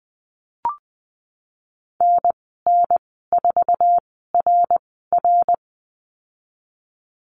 Morse code, heard as a keyed tone, about 20 wpm, spaced as for 15 wpm, 700 Hz. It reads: NN4RR
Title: 062 Lesson 11g - Letter h - Call Signs 20wpm